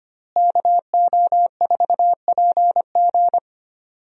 Morse code, heard as a keyed tone, about 25 wpm, 700 Hz. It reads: KO4PG
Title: USA Call Signs - 5 Characters - Repeated 3x - 25wpm